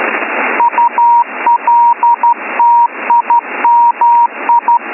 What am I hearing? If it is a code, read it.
Morse code, heard as a keyed tone, about 13 wpm, 950 Hz, with radio noise. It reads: ULTIMI